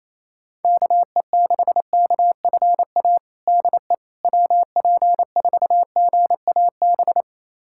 Morse code, heard as a keyed tone, about 28 wpm, 700 Hz. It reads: KE6KFA DE WP4GAB
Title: USA Callsign DE Callsign - 6 Characters - 28wpm